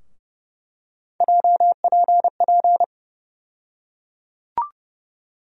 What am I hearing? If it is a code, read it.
Morse code, heard as a keyed tone, about 30 wpm, 700 Hz. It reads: JPP
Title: USA Call Sign Postfixes - 3 Letters 30wpm